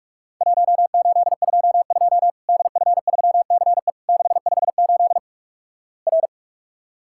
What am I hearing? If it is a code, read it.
Morse code, heard as a keyed tone, about 48 wpm, 700 Hz. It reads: 1822 DF3CE 658 R
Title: Worked All Europe DX Contest - QTC Exchange - 45wpm